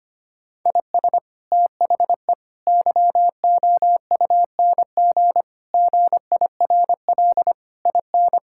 Morse code, heard as a keyed tone, about 25 wpm, 700 Hz. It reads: IS THE YOUNG GIRL IN
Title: Sentences from Top 1000 Words - Review Part 5 - 25wpm